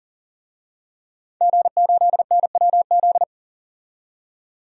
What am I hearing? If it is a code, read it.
Morse code, heard as a keyed tone, about 40 wpm, 700 Hz. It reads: G8NWZ
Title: DXCC-100 Call Signs - 5 Characters - 40wpm